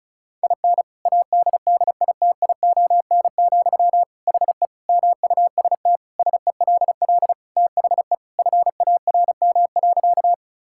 Morse code, heard as a keyed tone, about 36 wpm, 700 Hz. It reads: IN ADDITION, HE MUST SELL THE FARM.
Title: Sentences from Top 1000 Words - Review Part 6 - 35wpm